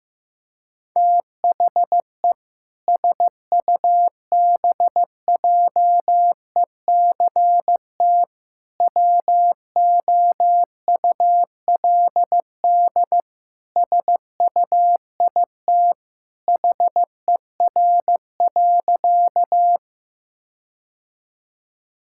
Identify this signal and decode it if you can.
Morse code, heard as a keyed tone, about 16 wpm, 700 Hz. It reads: THE SUBJECT WOULD SUIT HER.
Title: Sentences from Top 1600 Words 15wpm